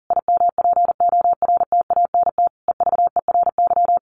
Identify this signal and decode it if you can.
Morse code, heard as a keyed tone, about 40 wpm, 700 Hz. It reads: IMPORTANT EVERY
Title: Sentences from Top 1000 Words 40wpm